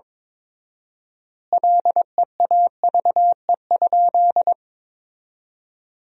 Morse code, heard as a keyed tone, about 22 wpm, 700 Hz. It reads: LEAVE?
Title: 4x Word-Spacing - Sentences from Top 400 Words - 22wpm